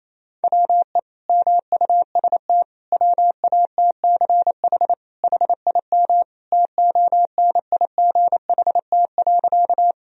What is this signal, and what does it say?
Morse code, heard as a keyed tone, about 28 wpm, 690 Hz. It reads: WE MUST WATCH HIM TONIGHT.
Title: Sentences from Top 1000 Words - Review Part 3 - 28wpm